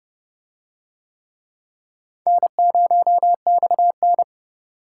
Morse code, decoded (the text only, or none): N0XN